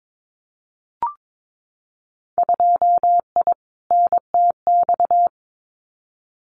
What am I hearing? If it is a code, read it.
Morse code, heard as a keyed tone, about 22 wpm, 700 Hz. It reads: 2I NTX
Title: Winter Field Day Exchanges Encore 22wpm